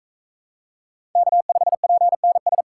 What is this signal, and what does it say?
Morse code, heard as a keyed tone, about 41 wpm, 700 Hz. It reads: K5PNS